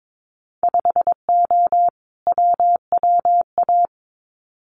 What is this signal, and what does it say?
Morse code, heard as a keyed tone, about 22 wpm, 700 Hz. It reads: 5O WWA
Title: Winter Field Day Exchanges Encore 22wpm